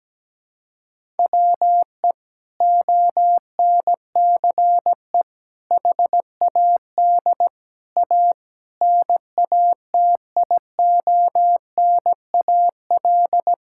Morse code, heard as a keyed tone, about 17 wpm, 700 Hz. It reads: WE ONCE HAD A NATIONAL
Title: Sentences from Top 1000 Words - Review Part 6 - 17wpm